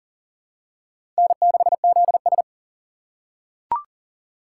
Morse code, heard as a keyed tone, about 40 wpm, 700 Hz. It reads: N6ZS
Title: USA Call Signs - 4 Characters - 4x Character Spacing - 40wpm